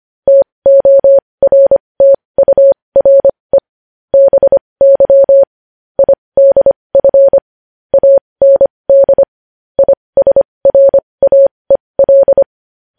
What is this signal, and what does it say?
Morse code, heard as a keyed tone, about 25 wpm, spaced as for 20 wpm, 550 Hz. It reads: TORTURE BY IDF AND ISRAEL
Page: news headlines in morse code at 20 wpm